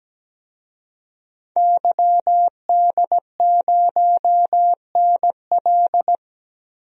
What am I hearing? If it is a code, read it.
Morse code, heard as a keyed tone, about 17 wpm, 700 Hz. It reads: YD0NL
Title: DXCC-100 Call Signs - 5 Characters - Repeated 2x - 17wpm